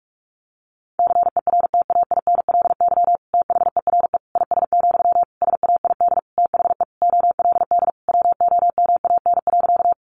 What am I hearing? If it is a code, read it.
Morse code, heard as a keyed tone, about 44 wpm, 700 Hz. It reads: CERTAINLY THERE IS, SAID THE OLD WOMAN.
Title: Sentences from Top 1100 Words 45wpm